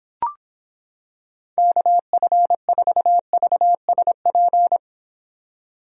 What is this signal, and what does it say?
Morse code, heard as a keyed tone, about 26 wpm, 700 Hz. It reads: KF4VSP